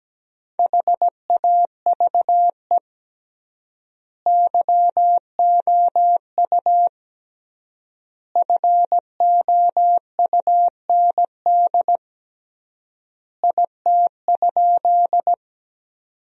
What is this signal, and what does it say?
Morse code, heard as a keyed tone, about 17 wpm, 700 Hz. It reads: HAVE YOU FOUND IT?
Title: 3x Word-Spacing - Sentences from Top 300 Words - 17wpm